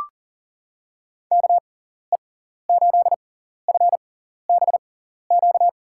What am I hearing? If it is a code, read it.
Morse code, heard as a keyed tone, about 37 wpm, spaced as for 14 wpm, 700 Hz. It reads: KE8FBQ
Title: USA Call Signs - 6 Characters - 6x Character Spacing - 40wpm